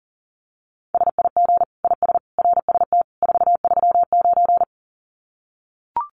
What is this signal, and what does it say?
Morse code, heard as a keyed tone, about 40 wpm, 700 Hz. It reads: SIG IS RST 439
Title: Common QSO Phrases 40wpm